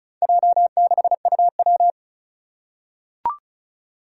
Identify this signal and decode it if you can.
Morse code, heard as a keyed tone, about 36 wpm, 700 Hz. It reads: J6UW
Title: Four Challenging Letter-Number Combinations 35wpm